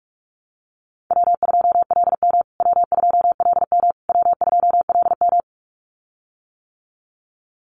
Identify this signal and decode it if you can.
Morse code, heard as a keyed tone, about 44 wpm, 700 Hz. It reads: W2LM W2LM W2LM
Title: USA Call Signs Repeated 3x 45wpm